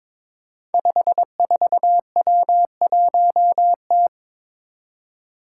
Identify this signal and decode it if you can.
Morse code, heard as a keyed tone, about 22 wpm, 700 Hz. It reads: 54W1T
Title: Five Challenging Letter-Number Combinations 22wpm